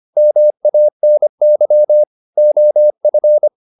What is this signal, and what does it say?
Morse code, heard as a keyed tone, about 25 wpm, 600 Hz. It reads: MANY OF
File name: the old man and the sea part 1 25wpm